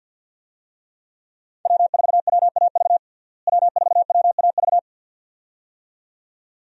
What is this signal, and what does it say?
Morse code, heard as a keyed tone, about 47 wpm, 700 Hz. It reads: W4WAV W4WAV
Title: USA Call Signs - 5 Characters - Repeated 2x - 50wpm